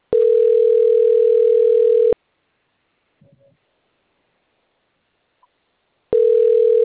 Ringback tone (North America), 440 + 480 Hz.